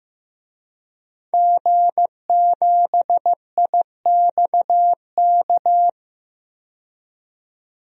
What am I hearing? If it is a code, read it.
Morse code, heard as a keyed tone, about 15 wpm, 700 Hz. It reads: G7IXK
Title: DXCC-100 Call Signs - 5 Characters - 15wpm